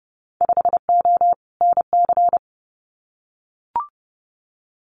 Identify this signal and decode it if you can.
Morse code, heard as a keyed tone, about 30 wpm, 700 Hz. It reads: 5O NC
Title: Winter Field Day Exchanges 30wpm